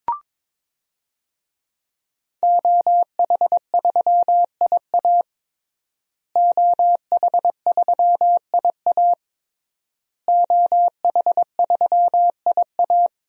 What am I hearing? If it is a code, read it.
Morse code, heard as a keyed tone, about 22 wpm, 700 Hz. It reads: OH3IA OH3IA OH3IA
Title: DXCC-100 Call Signs - 5 Characters - Repeated 3x - 22wpm